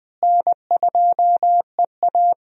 Morse code, heard as a keyed tone, about 20 wpm, 700 Hz. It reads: N2EA
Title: USA Call Signs - 4 Characters - Repeated 3x - 20wpm